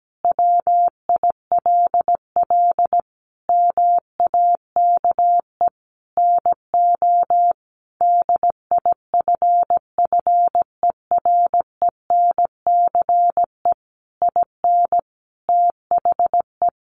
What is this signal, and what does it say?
Morse code, heard as a keyed tone, about 17 wpm, 700 Hz. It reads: WILL MAKE NO DIFFERENCE IN THE
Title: Sentences from Top 800 Words 17wpm